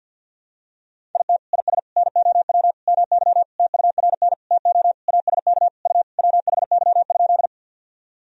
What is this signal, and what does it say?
Morse code, heard as a keyed tone, about 52 wpm, 700 Hz. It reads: IT IS NOW MY TURN TO ASK U WHY?
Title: Sentences from Top 200 Words and Abbreviations 50wpm